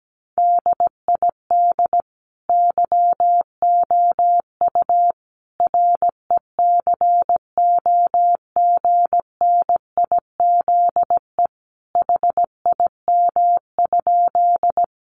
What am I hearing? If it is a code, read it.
Morse code, heard as a keyed tone, about 17 wpm, 700 Hz. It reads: DID YOU RECOGNIZE HIM?
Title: Sentences from Top 1400 Words 17wpm